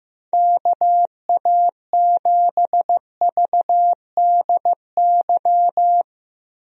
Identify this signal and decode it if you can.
Morse code, heard as a keyed tone, about 15 wpm, 700 Hz. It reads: KA7VDY
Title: USA Call Signs - 6 Characters - Repeated 2x - 15wpm